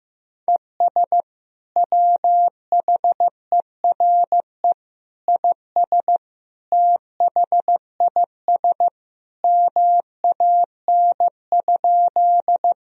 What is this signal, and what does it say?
Morse code, heard as a keyed tone, about 16 wpm, 700 Hz. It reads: ES WHERE IS THIS MAN?
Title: Sentences from Top 200 Words and Abbreviations 15wpm